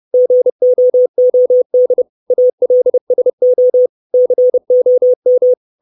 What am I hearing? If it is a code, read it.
Morse code, heard as a keyed tone, about 30 wpm, 500 Hz. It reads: GOOD ALSO COM
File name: 632 Words 30WPM